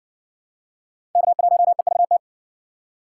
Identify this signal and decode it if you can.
Morse code, heard as a keyed tone, about 59 wpm, 700 Hz. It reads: K1EVT